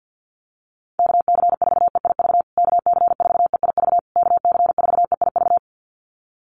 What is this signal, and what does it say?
Morse code, heard as a keyed tone, about 50 wpm, 700 Hz. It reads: KC4EIV KC4EIV KC4EIV
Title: USA Call Signs Repeated 3x 50wpm